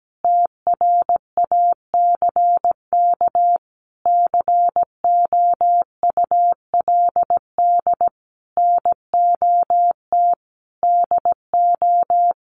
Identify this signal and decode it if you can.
Morse code, heard as a keyed tone, about 17 wpm, 700 Hz. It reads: TRACK COULD NOT DO
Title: Sentences from Top 1100 Words 17wpm